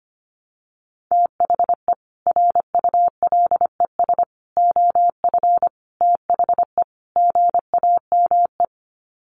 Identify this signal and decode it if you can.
Morse code, heard as a keyed tone, about 25 wpm, 700 Hz. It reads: THE RULES OF THE GAME
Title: Sets of 5 Words Encore2 25wpm